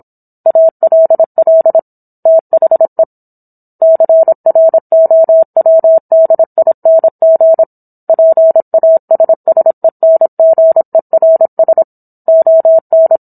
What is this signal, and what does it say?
Morse code, heard as a keyed tone, about 27 wpm, 650 Hz. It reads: ALL THE CROWDING PASSENGERS ON